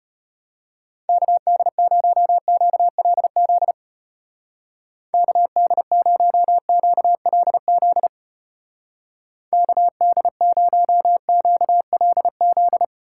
Morse code, heard as a keyed tone, about 32 wpm, 700 Hz. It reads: KD0QLZ KD0QLZ KD0QLZ